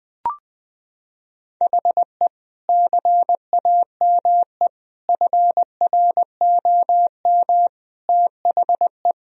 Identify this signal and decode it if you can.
Morse code, heard as a keyed tone, about 21 wpm, 700 Hz. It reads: HE CAME FROM THE
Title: Sentences from Top 1000 Words - Review Part 2 - 20wpm